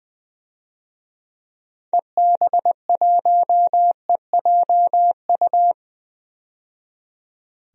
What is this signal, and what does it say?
Morse code, heard as a keyed tone, about 21 wpm, 700 Hz. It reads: EB1EJU
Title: DXCC-100 Call Signs - 6 Characters - 20wpm